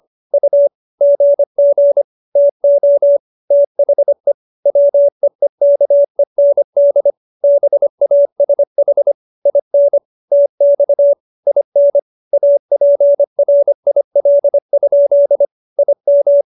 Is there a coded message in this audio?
Yes: Morse code from 0.3 to 16.4 s, heard as a keyed tone, about 25 wpm, 570 Hz. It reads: U GG TO THE WEEKEND BASH IN TX IN APRIL? IM